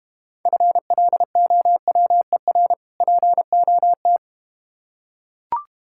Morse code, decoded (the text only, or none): FLOWER POT